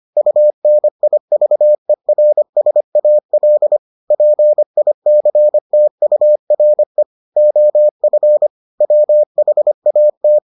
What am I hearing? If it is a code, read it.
Morse code, heard as a keyed tone, about 25 wpm, 600 Hz. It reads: UNIVERSAL PICTURE OF WHAT